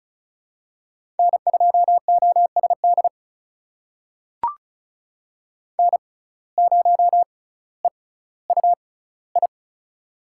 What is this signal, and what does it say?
Morse code, heard as a keyed tone, about 36 wpm, spaced as for 18 wpm, 700 Hz. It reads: N2OSD N0EUI
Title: USA Call Signs - 5 Characters - 6x Character Spacing - 35wpm